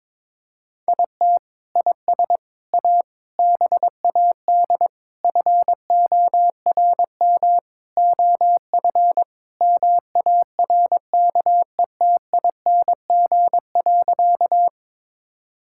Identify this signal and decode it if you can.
Morse code, heard as a keyed tone, about 22 wpm, 700 Hz. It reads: IT IS A BAD FORM OF MARKETING.
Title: Sentences from Top 1000 Words - Review Part 5 - 22wpm